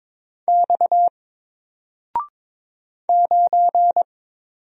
Morse code, heard as a keyed tone, about 22 wpm, 700 Hz. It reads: X 9
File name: Single-Letter-Number-22wpm